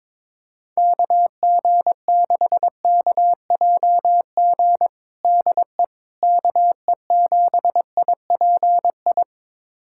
Morse code, heard as a keyed tone, about 22 wpm, 700 Hz. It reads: KG6KJG DE KE7IPI
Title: USA Callsign DE Callsign - 6 Characters - 22wpm